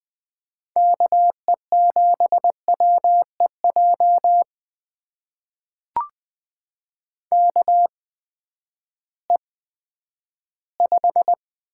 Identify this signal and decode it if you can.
Morse code, heard as a keyed tone, about 20 wpm, spaced as for 11 wpm, 700 Hz. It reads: KE7WEJ KE5
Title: USA Call Signs - 6 Characters - 8x Character Spacing - 20wpm